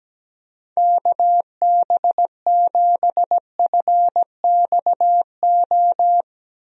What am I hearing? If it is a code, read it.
Morse code, heard as a keyed tone, about 17 wpm, 700 Hz. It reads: KB7FXO
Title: USA Call Signs - 6 Characters - Repeated 2x - 17wpm